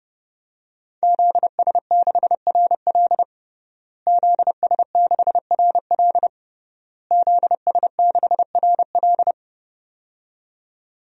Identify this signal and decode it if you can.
Morse code, heard as a keyed tone, about 30 wpm, 700 Hz. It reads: ZS6RL ZS6RL ZS6RL
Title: DXCC-100 Call Signs - 5 Characters - Repeated 3x - 30wpm